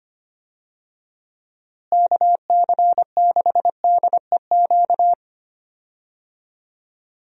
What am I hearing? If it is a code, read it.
Morse code, heard as a keyed tone, about 25 wpm, 700 Hz. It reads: KC6DEQ
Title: USA Call Signs - 6 Characters - Repeated 2x - 25wpm